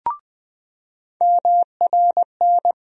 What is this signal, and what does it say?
Morse code, heard as a keyed tone, about 20 wpm, 700 Hz. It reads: MRN
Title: USA Call Sign Postfixes - 3 Letters 20wpm